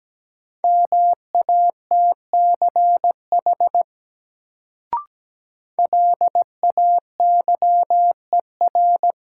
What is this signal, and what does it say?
Morse code, heard as a keyed tone, about 17 wpm, 700 Hz. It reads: MATCH LAYER
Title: Sentences from Top 1600 Words - New Word Drill - 17wpm